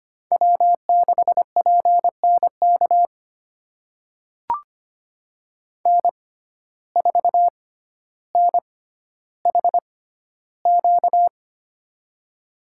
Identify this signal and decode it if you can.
Morse code, heard as a keyed tone, about 25 wpm, spaced as for 13 wpm, 700 Hz. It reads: W6PNK N4NHQ
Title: USA Call Signs - 5 Characters - 6x Character Spacing - 25wpm